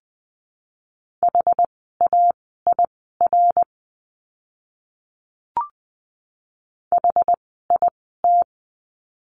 Morse code, heard as a keyed tone, about 20 wpm, spaced as for 15 wpm, 700 Hz. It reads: HAIR HIT
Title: 101 Lesson 17f - Letter u - Words 20wpm